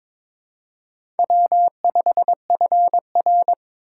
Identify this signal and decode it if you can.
Morse code, heard as a keyed tone, about 23 wpm, 700 Hz. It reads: W5FR
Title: USA Call Signs - 4 Characters - 3x Character Spacing - 22wpm